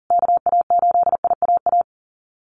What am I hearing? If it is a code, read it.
Morse code, heard as a keyed tone, about 37 wpm, 700 Hz. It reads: KA8IAA
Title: USA Call Signs Repeated 3x 40wpm